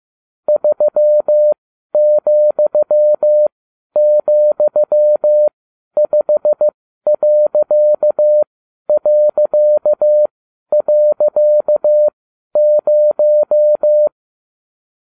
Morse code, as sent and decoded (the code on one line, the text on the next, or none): ...-- --..-- --..-- ..... .-.-.- .-.-.- .-.-.- -----
3,,5...0